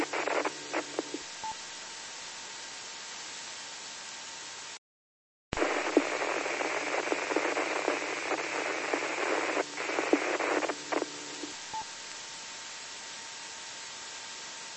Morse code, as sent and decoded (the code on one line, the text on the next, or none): none